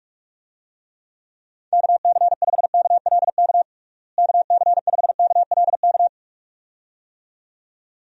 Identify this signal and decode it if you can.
Morse code, heard as a keyed tone, about 41 wpm, 700 Hz. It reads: KC5KLK KC5KLK
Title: USA Call Signs - 6 Characters - Repeated 2x - 45wpm